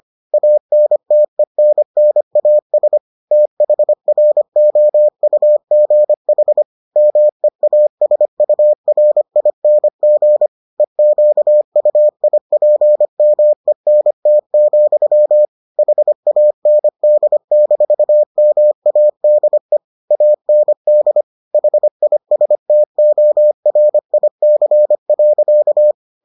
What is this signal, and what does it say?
Morse code, heard as a keyed tone, about 25 wpm, 600 Hz. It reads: ANTENNAS THROUGH MEASURING EQUIPMENT, HAND-MADE AND HISTORIC.